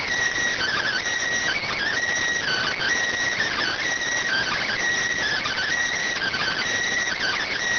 Emergency Alert System (SAME) tones (header bursts), heard as harsh buzzing bursts. It is heard from 0.6 to 6.6 s.